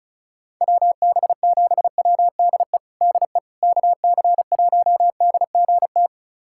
Morse code, heard as a keyed tone, about 36 wpm, 700 Hz. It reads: WB7WDE DE KC1DGT
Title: USA Callsign DE Callsign - 6 Characters - 35wpm